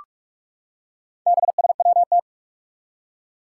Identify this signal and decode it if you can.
Morse code, heard as a keyed tone, about 44 wpm, 700 Hz. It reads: BSWT
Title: Four Challenging Letter-Number Combinations 45wpm